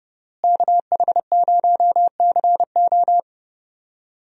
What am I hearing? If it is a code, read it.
Morse code, heard as a keyed tone, about 32 wpm, 700 Hz. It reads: KH0CO
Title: USA Call Signs - 5 Characters - Repeated 2x - 30wpm